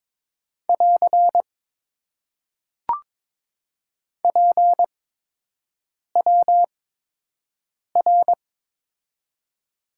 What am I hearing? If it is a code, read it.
Morse code, heard as a keyed tone, about 21 wpm, spaced as for 6 wpm, 700 Hz. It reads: <AR> PWR
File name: Top 90 QSO Elements - 8x Character Spacing - 22wpm